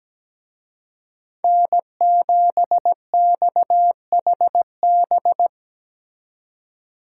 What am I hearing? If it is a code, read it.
Morse code, heard as a keyed tone, about 20 wpm, 700 Hz. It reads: N7XHB
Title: Speed Racing - USA Call Signs 17wpm